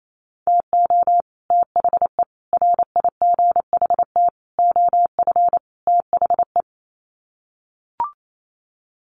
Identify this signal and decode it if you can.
Morse code, heard as a keyed tone, about 28 wpm, 700 Hz. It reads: TO THE RIGHT OF THE
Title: Sets of 5 Words Encore 28wpm